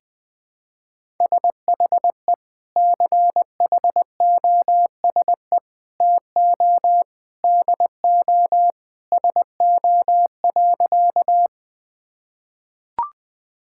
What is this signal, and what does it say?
Morse code, heard as a keyed tone, about 19 wpm, 700 Hz. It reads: SHE CHOSE TO DO SO.
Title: Sentences with 3-Word Phrases - 20wpm